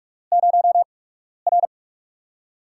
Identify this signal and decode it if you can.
Morse code, heard as a keyed tone, about 44 wpm, 700 Hz. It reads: 0 R
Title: USA Call Signs - 4 Characters - 8x Character Spacing - 45wpm